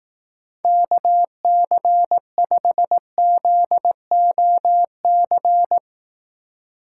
Morse code, heard as a keyed tone, about 18 wpm, 700 Hz. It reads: KC5ZOC
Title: Speed Racing - USA Call Signs 15wpm